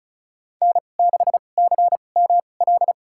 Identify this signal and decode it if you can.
Morse code, heard as a keyed tone, about 36 wpm, spaced as for 25 wpm, 700 Hz. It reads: N6CML
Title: USA Call Signs - 5 Characters - 2x Character Spacing - 35wpm